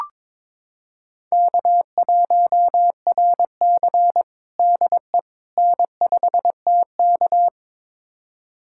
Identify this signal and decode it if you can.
Morse code, heard as a keyed tone, about 22 wpm, 700 Hz. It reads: K1RC DE N5TK
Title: USA Callsign DE Callsign - 4 Characters - 22wpm